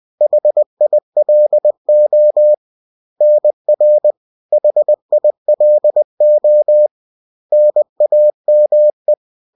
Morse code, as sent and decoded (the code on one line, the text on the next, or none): .... .. .-.. --- / -. .-. / .... .. .-.. --- / -. .- -- .
HILO NR HILO NAME